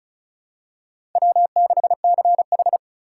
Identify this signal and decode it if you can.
Morse code, heard as a keyed tone, about 36 wpm, 700 Hz. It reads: W6CH
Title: USA Call Signs - 3x Character Spacing - 35wpm